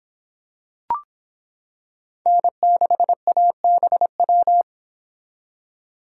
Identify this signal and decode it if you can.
Morse code, heard as a keyed tone, about 26 wpm, 700 Hz. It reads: N6ABW